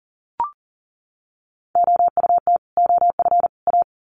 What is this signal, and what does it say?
Morse code, heard as a keyed tone, about 40 wpm, 700 Hz. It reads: OUT OF A
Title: Sets of 3 Words 40wpm